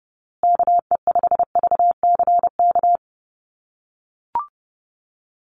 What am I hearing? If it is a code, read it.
Morse code, heard as a keyed tone, about 30 wpm, 700 Hz. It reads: KE5VCK
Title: USA Call Signs Repeated 2x 30wpm